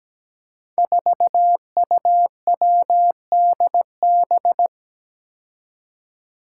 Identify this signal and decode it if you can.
Morse code, heard as a keyed tone, about 17 wpm, 700 Hz. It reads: 4UWDB
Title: Five Challenging Letter-Number Combinations 17wpm